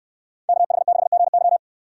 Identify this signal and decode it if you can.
Morse code, heard as a keyed tone, about 63 wpm, 700 Hz. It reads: KH6GY